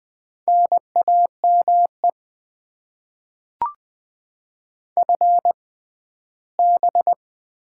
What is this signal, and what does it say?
Morse code, heard as a keyed tone, about 20 wpm, spaced as for 14 wpm, 700 Hz. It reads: NAME FB
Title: Top 20 QSO Elements - 6x Character Spacing - 20wpm